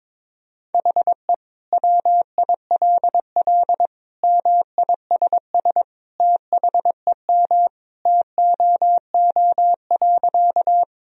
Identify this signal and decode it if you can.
Morse code, heard as a keyed tone, about 22 wpm, 700 Hz. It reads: HE WILL MISS THEM TOO.
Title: Sentences from Top 1000 Words - Review Part 3 - 22wpm